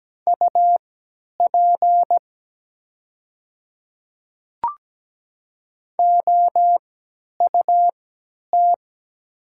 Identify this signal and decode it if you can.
Morse code, heard as a keyed tone, about 17 wpm, spaced as for 10 wpm, 700 Hz. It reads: UP OUT
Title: 3x Character Spacing - Top 500 Words - 17wpm